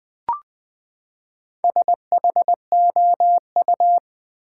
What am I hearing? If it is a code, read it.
Morse code, heard as a keyed tone, about 20 wpm, 700 Hz. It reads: SHOU
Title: Build A Word - 4 Letters At A Time - 8-Letter Words - 20wpm